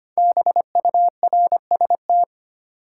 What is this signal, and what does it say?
Morse code, heard as a keyed tone, about 25 wpm, 700 Hz. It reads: BURST
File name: 5 Letter Words - First 3 Letters Given - 25wpm